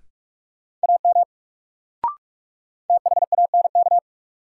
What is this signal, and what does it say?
Morse code, heard as a keyed tone, about 41 wpm, 700 Hz. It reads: AM THANK
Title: Sentences from Top 600 Words - New Word Drill - 45wpm